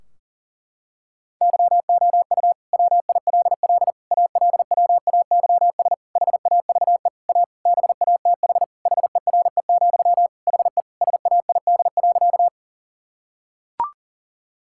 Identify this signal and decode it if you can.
Morse code, heard as a keyed tone, about 40 wpm, 700 Hz. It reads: YOU WILL ALWAYS HAVE A BATH HERE, HE SAID.
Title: Sentences from Top 1600 Words 40wpm